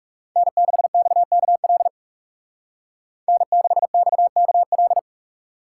Morse code, heard as a keyed tone, about 35 wpm, 700 Hz. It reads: N6XKL N6XKL
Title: Speed Racing - USA 1x3 Call Signs 30wpm